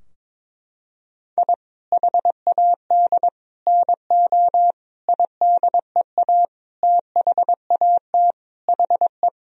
Morse code, heard as a keyed tone, about 22 wpm, 700 Hz. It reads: I HAD NO IDEA THAT HE
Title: Sentences from Top 1000 Words - Review Part 2 - 22wpm